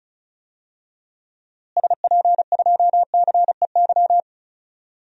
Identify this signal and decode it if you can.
Morse code, heard as a keyed tone, about 34 wpm, 700 Hz. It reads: SP2CEY
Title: DXCC-100 Call Signs - 6 Characters - Repeated 3x - 35wpm